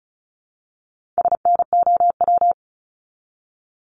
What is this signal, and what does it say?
Morse code, heard as a keyed tone, about 35 wpm, 700 Hz. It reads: SNOW
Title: Multi-letter Phonemes Repeated 3x 35wpm